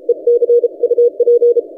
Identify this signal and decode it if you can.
Morse code, heard as a keyed tone, about 32 wpm, 480 Hz. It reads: EC UP
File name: posts / vk9ec-40cw- -1410z